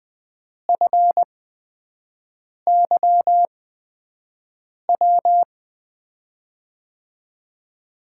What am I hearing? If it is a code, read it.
Morse code, heard as a keyed tone, about 20 wpm, spaced as for 5 wpm, 700 Hz. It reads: FYW
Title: USA Call Signs - 6 Characters - 8x Character Spacing - 20wpm